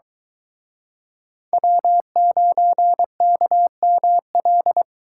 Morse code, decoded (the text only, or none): W9KML